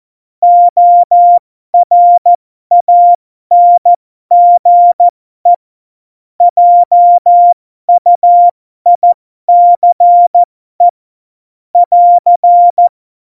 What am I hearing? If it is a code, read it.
Morse code, heard as a keyed tone, about 14 wpm, spaced as for 12 wpm, 700 Hz. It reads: ORANGE JUICE <AR>
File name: Morsekurs / Witze